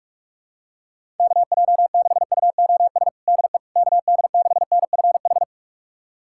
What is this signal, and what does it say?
Morse code, heard as a keyed tone, about 46 wpm, 700 Hz. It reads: KJ6UOS DE KD6NFH
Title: USA Callsign DE Callsign - 6 Characters - 45wpm